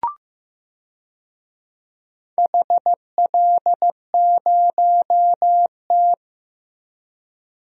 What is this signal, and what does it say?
Morse code, heard as a keyed tone, about 15 wpm, 700 Hz. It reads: HL0T
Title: DXCC-100 Call Signs - 4 Characters - Repeated 3x - 15wpm